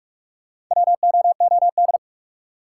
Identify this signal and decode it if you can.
Morse code, heard as a keyed tone, about 44 wpm, 700 Hz. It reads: WOOD
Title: Sentences from Top 1000 Words - Review Part 2 - New Word Drill - 45wpm